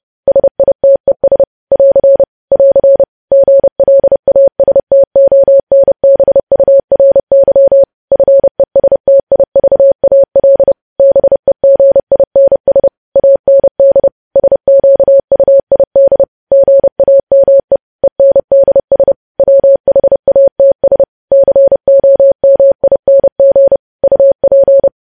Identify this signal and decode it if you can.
Morse code, heard as a keyed tone, about 30 wpm, 550 Hz. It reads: SITES <AR> <AR> GLASTONBURY FESTIVAL BEGINS AND SQUID GAME ENDS WHATS COMING UP
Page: news headlines in morse code at 30 wpm